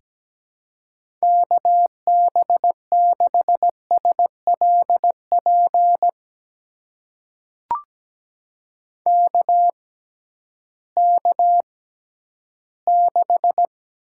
Morse code, heard as a keyed tone, about 17 wpm, spaced as for 11 wpm, 700 Hz. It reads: KB6SLP KK6
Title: USA Call Signs - 6x Character Spacing - 17wpm